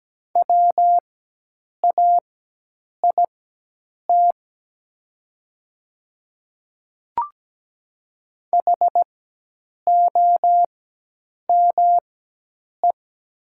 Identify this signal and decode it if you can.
Morse code, heard as a keyed tone, about 17 wpm, spaced as for 8 wpm, 700 Hz. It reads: WAIT HOME
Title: 4x Character Spacing - Top 500 Words - 17wpm